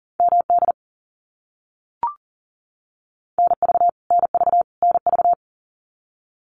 Morse code, heard as a keyed tone, about 40 wpm, 700 Hz. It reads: MD NV NV NV
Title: US States Repeated 3x 40wpm